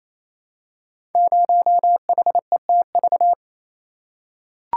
Morse code, heard as a keyed tone, about 27 wpm, 700 Hz. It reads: 0HETV
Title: Code Groups - 5 Letter:Numbers - 28wpm